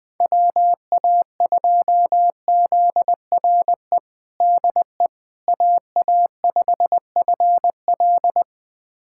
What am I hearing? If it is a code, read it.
Morse code, heard as a keyed tone, about 20 wpm, 700 Hz. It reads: WA2ZRE DE AA5FL
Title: USA Callsign DE Callsign 20wpm